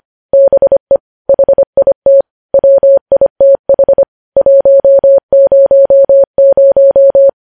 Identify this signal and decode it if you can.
Morse code, heard as a keyed tone, about 25 wpm, 550 Hz. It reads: BE HIT WITH 100
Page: news-headlines-in-morse-code-at-25-wpm